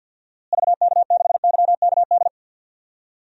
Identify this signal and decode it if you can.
Morse code, heard as a keyed tone, about 51 wpm, 700 Hz. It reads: VK6YKD
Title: DXCC-100 Call Signs - 6 Characters - Repeated 2x - 50wpm